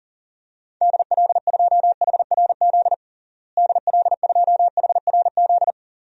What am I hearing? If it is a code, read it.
Morse code, heard as a keyed tone, about 40 wpm, 700 Hz. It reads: DL2HRZ DL2HRZ
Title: DXCC-100 Call Signs - 6 Characters - Repeated 3x - 40wpm